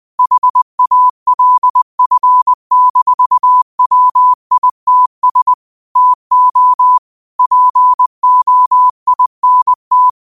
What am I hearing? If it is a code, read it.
Morse code, heard as a keyed tone, about 20 wpm, 1000 Hz. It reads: HALF-WITS TO POINT
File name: qotdfiles / 20wpm